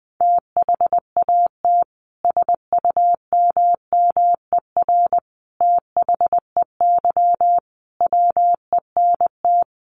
Morse code, heard as a keyed tone, about 20 wpm, 700 Hz. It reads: THAT SUMMER THEY WENT
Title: Sentences from Top 1100 Words 20wpm